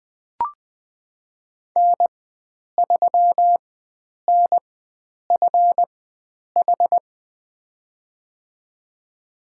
Morse code, heard as a keyed tone, about 21 wpm, spaced as for 9 wpm, 700 Hz. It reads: N3NFH
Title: USA Call Signs - 5 Characters - 4x Character Spacing - 20wpm